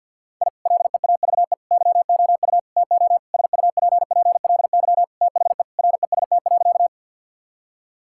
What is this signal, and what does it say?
Morse code, heard as a keyed tone, about 51 wpm, 700 Hz. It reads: I LEAVE YOU TO SUPPLY THE REST.
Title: Sentences from Top 1000 Words - Review Part 5 - 50wpm